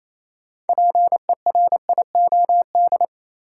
Morse code, heard as a keyed tone, about 27 wpm, 700 Hz. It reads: PERIOD
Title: Sentences from Top 800 Words - New Word Drill - 28wpm